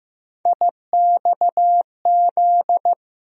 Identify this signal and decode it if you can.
Morse code, heard as a keyed tone, about 15 wpm, 700 Hz. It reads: IXZ
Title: USA Call Sign Postfixes - 3 Letters 15wpm